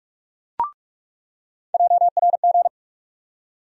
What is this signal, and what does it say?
Morse code, heard as a keyed tone, about 42 wpm, 700 Hz. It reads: JRG
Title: USA Call Sign Postfixes - 3 Letters 45wpm